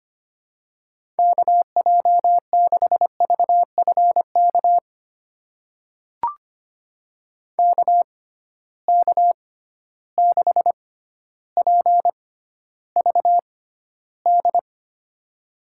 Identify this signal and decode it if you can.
Morse code, heard as a keyed tone, about 25 wpm, spaced as for 13 wpm, 700 Hz. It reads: KJ6VFK KK6PVD
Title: USA Call Signs - 6 Characters - 6x Character Spacing - 25wpm